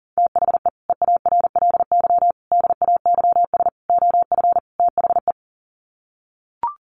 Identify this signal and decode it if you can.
Morse code, heard as a keyed tone, about 40 wpm, 700 Hz. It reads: THE EARLY DAYS OF THE
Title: Sets of 5 Words Encore 40wpm